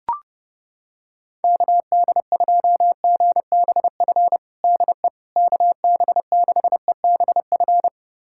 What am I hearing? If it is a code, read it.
Morse code, heard as a keyed tone, about 30 wpm, 700 Hz. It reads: KD2GBF DE KB6EBF
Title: USA Callsign DE Callsign 30wpm